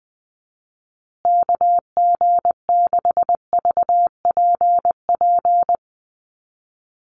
Morse code, heard as a keyed tone, about 20 wpm, 700 Hz. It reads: KG6VPP
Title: USA Call Signs Repeated 3x 20wpm